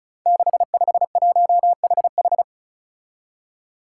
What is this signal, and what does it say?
Morse code, heard as a keyed tone, about 36 wpm, 700 Hz. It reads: 651HH
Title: Five Challenging Letter-Number Combinations 35wpm